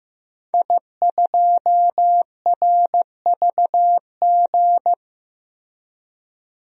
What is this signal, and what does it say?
Morse code, heard as a keyed tone, about 15 wpm, 700 Hz. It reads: I2RVG
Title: DXCC-100 Call Signs - 5 Characters - Repeated 3x - 15wpm